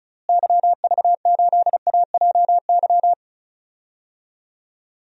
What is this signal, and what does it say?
Morse code, heard as a keyed tone, about 36 wpm, 700 Hz. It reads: YV8AJY
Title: DXCC-100 Call Signs - 6 Characters - Repeated 3x - 35wpm